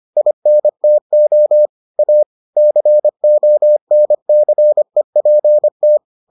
Morse code, heard as a keyed tone, about 25 wpm, 600 Hz. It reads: INTO A CONCEPT